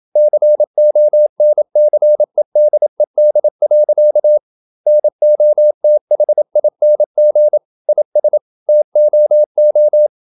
Morse code, heard as a keyed tone, about 27 wpm, 600 Hz. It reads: CONCEDED. NOTHING IS TOO
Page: Never 26wpm | CW med Gnister